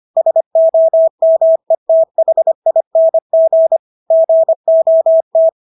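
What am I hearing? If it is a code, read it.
Morse code, heard as a keyed tone, about 25 wpm, 650 Hz. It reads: SOMETHING GOT
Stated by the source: Morse practice of English words at 25 wpm from CW med Gnister.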